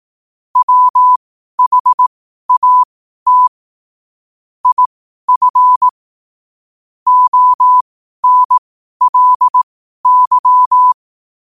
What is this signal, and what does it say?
Morse code, heard as a keyed tone, about 18 wpm, spaced as for 12 wpm, 1000 Hz. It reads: WHAT IF ONLY
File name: qotdfiles / 12wpm